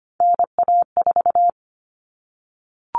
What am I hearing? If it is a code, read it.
Morse code, heard as a keyed tone, about 25 wpm, 700 Hz. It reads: NA4
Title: All Cut Numbers - 3-Digit 25wpm